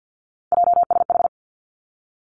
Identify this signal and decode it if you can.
Morse code, heard as a keyed tone, about 50 wpm, 700 Hz. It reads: JSH
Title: Three Challenging Letter-Number Combinations 50wpm